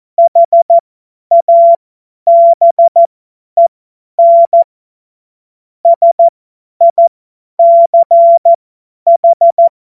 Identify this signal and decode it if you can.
Morse code, heard as a keyed tone, about 14 wpm, spaced as for 10 wpm, 680 Hz. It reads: HABEN SICH
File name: Morsekurs / Witze / Witz 20. Januar 2026 50 F70